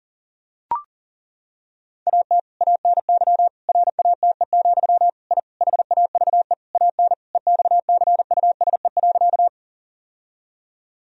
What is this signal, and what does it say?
Morse code, heard as a keyed tone, about 40 wpm, 700 Hz. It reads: AT ANY RATE, I HAVE AN EXCUSE.
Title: Sentences from Top 1000 Words - Review Part 4 - 40wpm